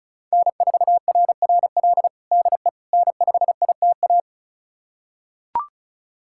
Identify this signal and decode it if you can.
Morse code, heard as a keyed tone, about 34 wpm, 700 Hz. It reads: N4RRL DE N5ITA
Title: USA Callsign DE Callsign - 5 Characters - 35wpm